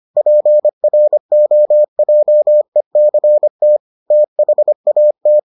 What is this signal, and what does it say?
Morse code, heard as a keyed tone, about 25 wpm, 600 Hz. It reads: PROJECT THAT